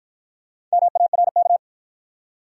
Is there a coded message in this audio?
Yes: Morse code from 0.7 to 1.6 s, heard as a keyed tone, about 52 wpm, 700 Hz. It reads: MARK